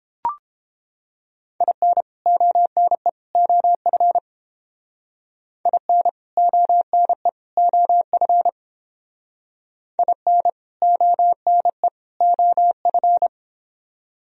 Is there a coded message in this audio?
Yes: Morse code from 1.6 to 13.3 s, heard as a keyed tone, about 29 wpm, 700 Hz. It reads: IN ONE OF IN ONE OF IN ONE OF